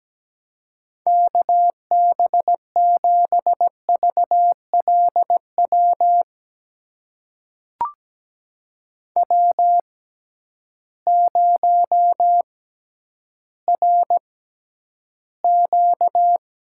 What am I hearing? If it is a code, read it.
Morse code, heard as a keyed tone, about 17 wpm, spaced as for 10 wpm, 700 Hz. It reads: KB7VLW W0RQ
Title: USA Call Signs - 6x Character Spacing - 17wpm